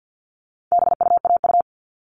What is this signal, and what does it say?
Morse code, heard as a keyed tone, about 50 wpm, 700 Hz. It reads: BUAU
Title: All Cut Numbers - 4-Digit 50wpm